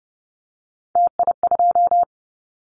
Morse code, heard as a keyed tone, about 30 wpm, 700 Hz. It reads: TI2
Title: Callsign Prefixes for Top 100 DXCC - 30wpm